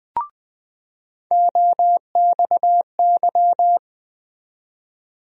Morse code, heard as a keyed tone, about 20 wpm, 700 Hz. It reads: OXY
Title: USA Call Sign Postfixes - 3 Letters 20wpm